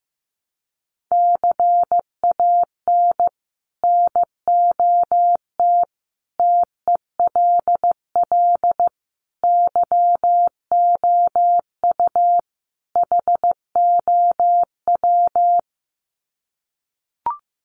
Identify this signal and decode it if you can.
Morse code, heard as a keyed tone, about 15 wpm, 700 Hz. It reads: CAN NOT TELL YOU HOW
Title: Sets of 5 Words Encore2 15wpm